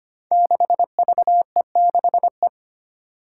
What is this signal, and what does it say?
Morse code, heard as a keyed tone, about 25 wpm, 700 Hz. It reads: 6VE6E
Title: Five Challenging Letter-Number Combinations 25wpm